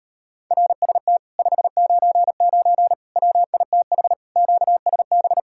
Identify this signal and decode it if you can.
Morse code, heard as a keyed tone, about 33 wpm, 700 Hz. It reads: RST 599 WITH QSB